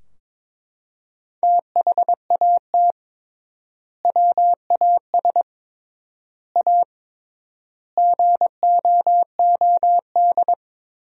Morse code, heard as a keyed tone, about 22 wpm, 700 Hz. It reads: THAT WAS A GOOD
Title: 3x Word-Spacing - Sentences from Top 200 Words - 22wpm